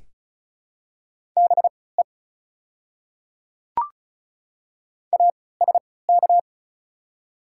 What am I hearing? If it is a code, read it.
Morse code, heard as a keyed tone, about 36 wpm, spaced as for 20 wpm, 700 Hz. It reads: BE ASK
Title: 3x Character Spacing - Top 200 Words - 35wpm